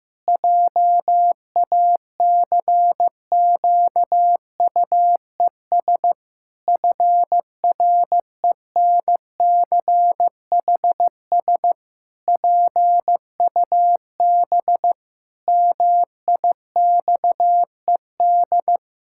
Morse code, heard as a keyed tone, about 15 wpm, 700 Hz. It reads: JACQUES FRENCHS PUB MIXED
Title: Pangram Challenge v2 - 15wpm